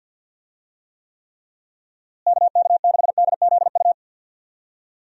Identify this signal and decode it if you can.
Morse code, heard as a keyed tone, about 51 wpm, 700 Hz. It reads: KK6DZU
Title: USA Call Signs - 6 Characters - Repeated 3x - 50wpm